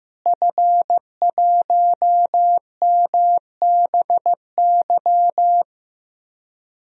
Morse code, heard as a keyed tone, about 15 wpm, 700 Hz. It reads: F1MBY
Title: DXCC-100 Call Signs - 5 Characters - Repeated 3x - 15wpm